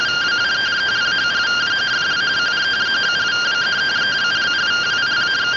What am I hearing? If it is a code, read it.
Two-tone teleprinter signal, about 170 Hz shift.